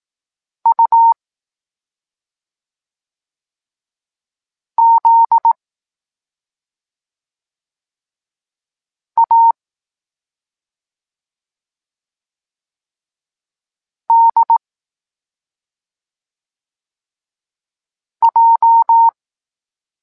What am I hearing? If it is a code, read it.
Morse code, heard as a keyed tone, about 18 wpm, 920 Hz. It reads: U Z A D J
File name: Tekst-23 4